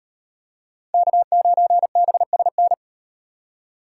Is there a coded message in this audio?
Yes: Morse code from 0.9 to 2.7 s, heard as a keyed tone, about 38 wpm, 700 Hz. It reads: K9BSN